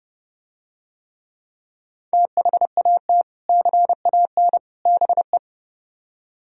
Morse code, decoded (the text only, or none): THAT CAN BE